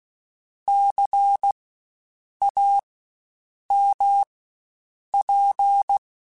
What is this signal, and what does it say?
Morse code, heard as a keyed tone, about 16 wpm, spaced as for 7 wpm, 800 Hz. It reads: CAMP
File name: Spaced Alphas / Spaced Alphas mp3s / Cs sp @15